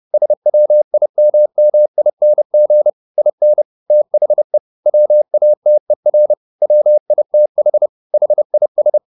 Morse code, heard as a keyed tone, about 30 wpm, 600 Hz. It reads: SWIMMING IN THE WATER WITH HIS
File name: the old man and the sea / the old man and the sea 1 30wpm